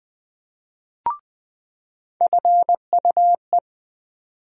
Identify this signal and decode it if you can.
Morse code, heard as a keyed tone, about 21 wpm, 700 Hz. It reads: FUE
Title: USA Call Sign Postfixes - 3 Letters 20wpm